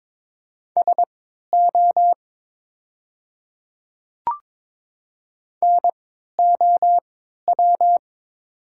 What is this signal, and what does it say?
Morse code, heard as a keyed tone, about 22 wpm, spaced as for 12 wpm, 700 Hz. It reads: SO NOW
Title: 3x Character Spacing - Top 100 Words - 22wpm